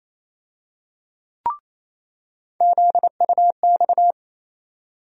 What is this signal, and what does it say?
Morse code, heard as a keyed tone, about 28 wpm, 700 Hz. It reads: ZUX
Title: USA Call Sign Postfixes - 3 Letters 28wpm